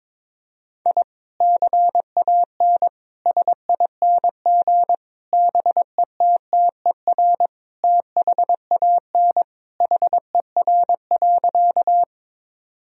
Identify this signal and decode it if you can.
Morse code, heard as a keyed tone, about 22 wpm, 700 Hz. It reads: I CAN SING BETTER THAN HER.
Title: Sentences from Top 1000 Words - Review Part 2 - 22wpm